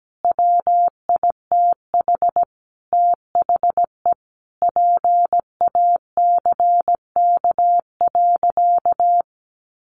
Morse code, heard as a keyed tone, about 17 wpm, 700 Hz. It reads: WITH THE PACK.
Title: Sentences from Top 1300 Words 17wpm